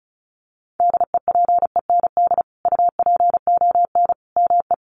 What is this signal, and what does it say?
Morse code, heard as a keyed tone, about 34 wpm, 700 Hz. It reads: DEPEND UPON ME
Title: Sentences from Top 1300 Words 35wpm